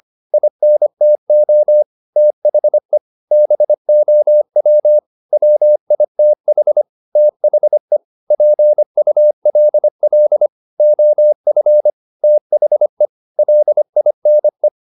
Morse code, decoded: INTO THE BOW WITH THE PULL OF THE LINE

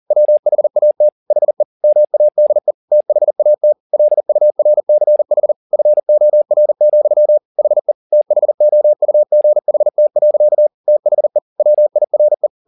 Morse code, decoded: WHAT HE MADE THAT LURCH FOR, HE THOUGHT. THE WIRE

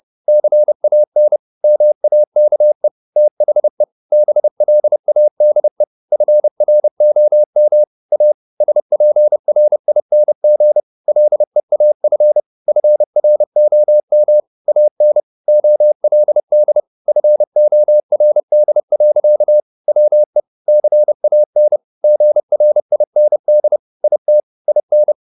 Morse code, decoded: CAN MAKE THE BLADE FROM A SPRING LEAF FROM AN OLD FORD. WE CAN GRIND IT IN

